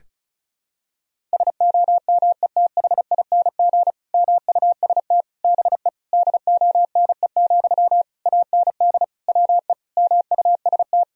Morse code, heard as a keyed tone, about 36 wpm, 700 Hz. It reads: SOMETHING MUST BE DONE, AND WE MUST
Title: Sentences from Top 1000 Words - Review Part 1 - 35wpm